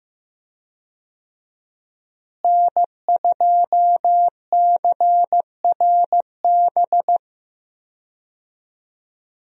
Morse code, heard as a keyed tone, about 15 wpm, 700 Hz. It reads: N2CRB
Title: USA Call Signs - 5 Characters - Repeated 2x - 15wpm